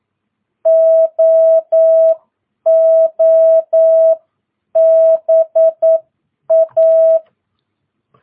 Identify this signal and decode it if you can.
Morse code, heard as a keyed tone, about 9 wpm, spaced as for 8 wpm, 660 Hz. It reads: OOBA
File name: jh8cbh / 2022cw